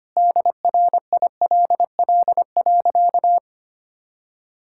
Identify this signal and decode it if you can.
Morse code, heard as a keyed tone, about 25 wpm, 700 Hz. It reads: DRILL.
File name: practice / 25wpm / Sentences with 3-Word Phrases - 25wpm